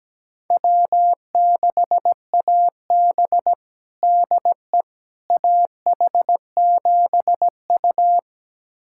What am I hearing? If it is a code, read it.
Morse code, heard as a keyed tone, about 17 wpm, 700 Hz. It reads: W6AB DE AH7U
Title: USA Callsign DE Callsign - 4 Characters - 17wpm